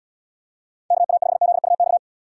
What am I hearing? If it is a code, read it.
Morse code, heard as a keyed tone, about 79 wpm, 700 Hz. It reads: KA5ZUX